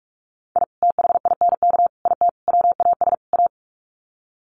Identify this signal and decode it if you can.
Morse code, heard as a keyed tone, about 44 wpm, 700 Hz. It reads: I THINK IT WAS A
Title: Sets of 5 Words 45wpm